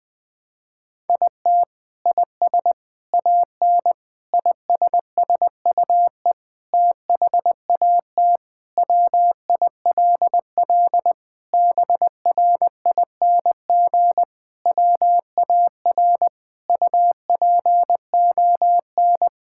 Morse code, heard as a keyed tone, about 20 wpm, 700 Hz. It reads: IT IS AN ISSUE THAT WILL BRING WAR UPON
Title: Sentences from Top 1000 Words - Review Part 6 - 20wpm